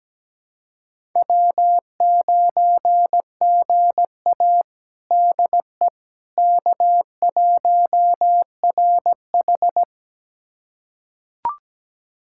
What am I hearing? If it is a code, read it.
Morse code, heard as a keyed tone, about 17 wpm, 700 Hz. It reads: W9GA DE K1RH
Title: USA Callsign DE Callsign - 4 Characters - 17wpm